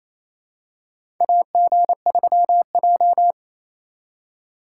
Morse code, heard as a keyed tone, about 28 wpm, 700 Hz. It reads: AG3J
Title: USA Call Signs - 4 Characters - 2x Character Spacing - 28wpm